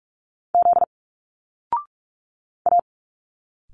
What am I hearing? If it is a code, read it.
Morse code, heard as a keyed tone, about 45 wpm, 700 Hz. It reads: Z A
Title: Single Letters 45wpm